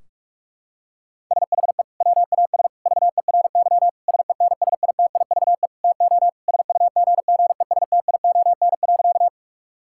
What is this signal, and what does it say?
Morse code, heard as a keyed tone, about 46 wpm, 700 Hz. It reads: SHE WAS VERY SENSITIVE TO SUGGESTION.
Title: Sentences from Top 1600 Words 45wpm